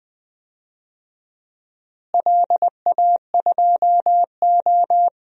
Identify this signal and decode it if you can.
Morse code, heard as a keyed tone, about 20 wpm, 700 Hz. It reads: LA2O
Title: DXCC-100 Call Signs - 4 Characters - Repeated 3x - 20wpm